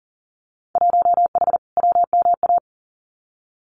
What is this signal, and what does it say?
Morse code, heard as a keyed tone, about 40 wpm, 700 Hz. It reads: 1H WMA
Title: Winter Field Day Exchanges Encore 40wpm